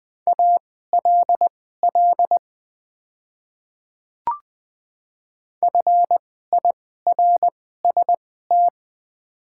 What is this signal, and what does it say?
Morse code, heard as a keyed tone, about 20 wpm, spaced as for 15 wpm, 690 Hz. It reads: ALL FIRST
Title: 2x Character Spacing - Top 100 Words - 20wpm